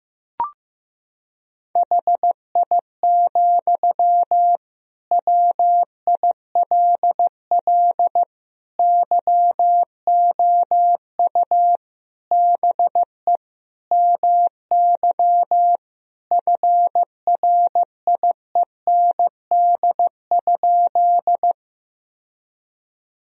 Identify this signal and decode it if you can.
Morse code, heard as a keyed tone, about 15 wpm, 700 Hz. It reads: HI, WILL YOU BE MY FRIEND?
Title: Sentences from Top 1000 Words - Review Part 3 - 15wpm